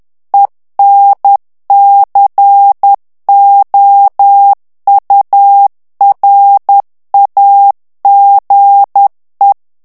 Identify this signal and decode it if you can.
Morse code, heard as a keyed tone, about 10 wpm, 800 Hz. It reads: ENCOURAGE
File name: ANGLAIS texte clair 3 vitesse 10 mots minute